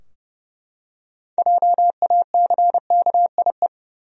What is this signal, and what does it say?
Morse code, heard as a keyed tone, about 32 wpm, 700 Hz. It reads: JACKIE
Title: USA Common Names - 30wpm